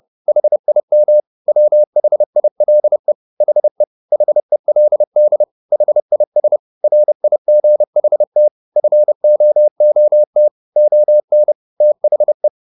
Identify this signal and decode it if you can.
Morse code, heard as a keyed tone, about 30 wpm, 600 Hz. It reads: HIM WHILE HE HELD HIS RIGHT FOOT ON THE